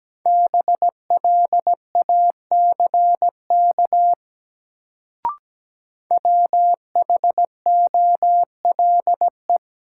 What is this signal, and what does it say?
Morse code, heard as a keyed tone, about 17 wpm, 700 Hz. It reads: BLACK WHOLE
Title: Sentences from Top 400 Words - New Word Drill - 17wpm